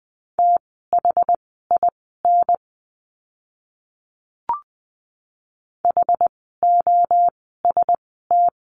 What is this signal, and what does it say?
Morse code, heard as a keyed tone, about 20 wpm, spaced as for 14 wpm, 700 Hz. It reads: THIN HOST
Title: 068 Lesson 12f - Letter d - Words 20wpm